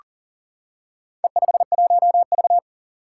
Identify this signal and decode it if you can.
Morse code, heard as a keyed tone, about 41 wpm, 700 Hz. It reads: E51V
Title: Four Challenging Letter-Number Combinations 40wpm